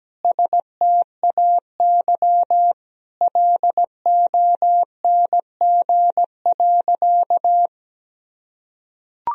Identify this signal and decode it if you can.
Morse code, heard as a keyed tone, about 17 wpm, 700 Hz. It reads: STAY LONG.
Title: Sentences from Top 1000 Words - Review Part 3 - 17wpm